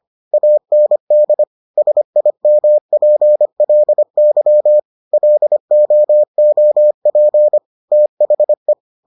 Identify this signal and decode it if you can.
Morse code, heard as a keyed tone, about 25 wpm, 600 Hz. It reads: AND SIMPLY LOOP THE